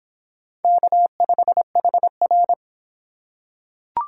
Morse code, heard as a keyed tone, about 24 wpm, 700 Hz. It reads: K5HR